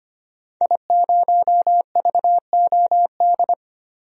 Morse code, heard as a keyed tone, about 25 wpm, 700 Hz. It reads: I0VOD
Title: DXCC-100 Call Signs - 5 Characters - 25wpm